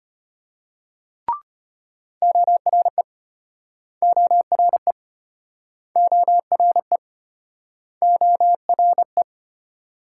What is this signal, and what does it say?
Morse code, heard as a keyed tone, about 35 wpm, 700 Hz. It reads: ORE ORE ORE ORE